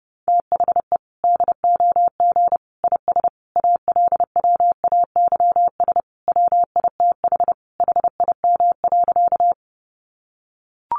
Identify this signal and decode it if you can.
Morse code, heard as a keyed tone, about 30 wpm, 700 Hz. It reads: THE DOG IS ALWAYS WITH HIM.
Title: Sentences from Top 400 Words 30wpm